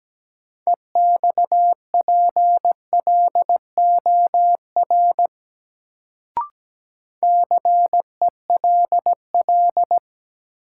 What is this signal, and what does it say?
Morse code, heard as a keyed tone, about 17 wpm, 700 Hz. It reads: EXPLOR CELL
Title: Sentences from Top 1100 Words - New Word Drill - 17wpm